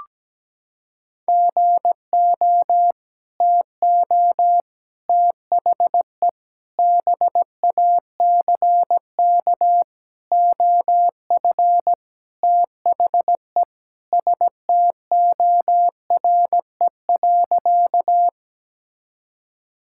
Morse code, heard as a keyed tone, about 17 wpm, 700 Hz. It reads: GO TO THE BACK OF THE STORE.
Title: Sentences from Top 1000 Words - Review Part 5 - 17wpm